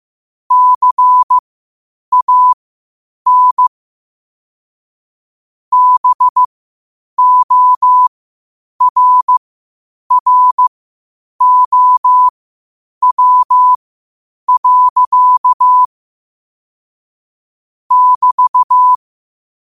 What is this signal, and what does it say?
Morse code, heard as a keyed tone, about 15 wpm, spaced as for 8 wpm, 1000 Hz. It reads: CAN BORROW. <BT>
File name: qotdfiles / 8wpm